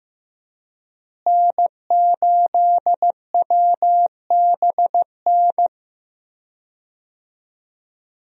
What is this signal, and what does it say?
Morse code, heard as a keyed tone, about 15 wpm, 700 Hz. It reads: N8WBN
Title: USA Call Signs - 5 Characters - Repeated 2x - 15wpm